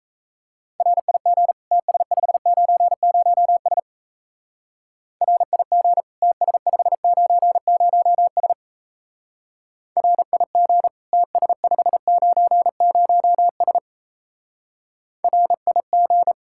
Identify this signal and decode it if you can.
Morse code, heard as a keyed tone, about 35 wpm, 700 Hz. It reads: RIG TS590S RIG TS590S RIG TS590S RIG
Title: Speed-Racing Common QSO Phrases - 28wpm